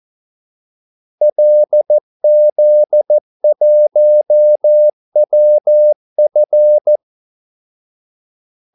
Morse code, heard as a keyed tone, about 14 wpm, 600 Hz. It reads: LZ1WF
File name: Callsigns-14wpm